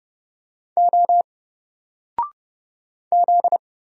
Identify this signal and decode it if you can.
Morse code, heard as a keyed tone, about 31 wpm, 700 Hz. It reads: O Z
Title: Single Letter-Number 30wpm